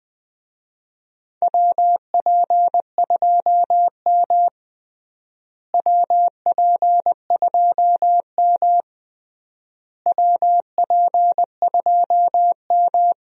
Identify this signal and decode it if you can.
Morse code, heard as a keyed tone, about 21 wpm, 700 Hz. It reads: WP2M WP2M WP2M
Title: USA Call Signs - 4 Characters - Repeated 3x - 20wpm